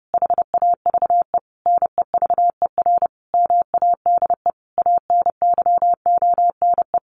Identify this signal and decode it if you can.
Morse code, heard as a keyed tone, about 30 wpm, 700 Hz. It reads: HAVE NEVER MADE ANYONE
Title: Sentences from Top 1100 Words 30wpm